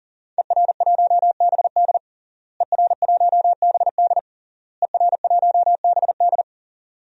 Morse code, heard as a keyed tone, about 37 wpm, 700 Hz. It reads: ER1BD ER1BD ER1BD
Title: DXCC-100 Call Signs - 5 Characters - Repeated 3x - 40wpm